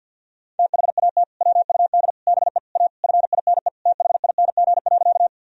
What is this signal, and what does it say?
Morse code, heard as a keyed tone, about 52 wpm, 700 Hz. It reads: THAT WUD BE A FINE THING.
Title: Sentences from Top 500 Words and Abbreviations 50wpm